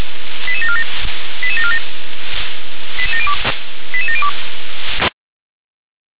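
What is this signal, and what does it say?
Five-tone selective call (ZVEI), heard as quick runs of five tones, tones of about 70 ms. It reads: 88747 88747 88702
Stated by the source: Five-tone ZVEI selective call received by SDR.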